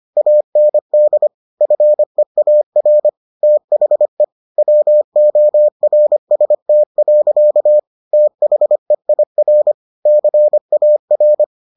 Morse code, heard as a keyed tone, about 25 wpm, 600 Hz. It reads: AND FEAR THE WORST. THEIR CAR